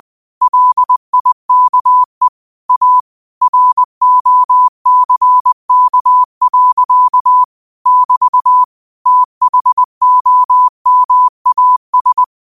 Morse code, heard as a keyed tone, about 20 wpm, 1000 Hz. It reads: LIKE A ROCK. <BT> THOMAS